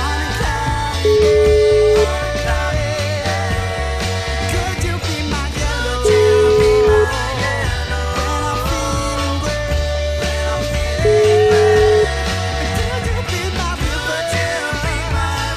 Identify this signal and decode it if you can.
Ringback tone (Europe, 425 Hz), with noise.